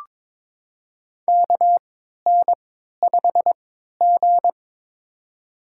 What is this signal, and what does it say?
Morse code, heard as a keyed tone, about 22 wpm, spaced as for 12 wpm, 700 Hz. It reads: KN5G
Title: USA Call Signs - 4 Characters - 3x Character Spacing - 22wpm